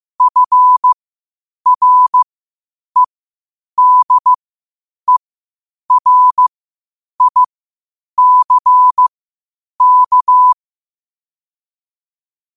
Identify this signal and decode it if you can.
Morse code, heard as a keyed tone, about 15 wpm, spaced as for 8 wpm, 1000 Hz. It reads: FREDERICK
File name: qotdfiles / 8wpm